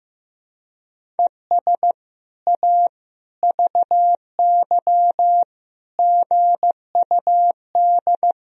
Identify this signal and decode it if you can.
Morse code, heard as a keyed tone, about 16 wpm, 700 Hz. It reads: ES A VY GUD
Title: Sentences from Top 200 Words and Abbreviations 15wpm